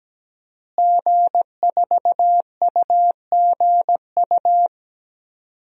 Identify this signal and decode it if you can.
Morse code, heard as a keyed tone, about 17 wpm, 700 Hz. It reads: G4UGU
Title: DXCC-100 Call Signs - 5 Characters - Repeated 2x - 17wpm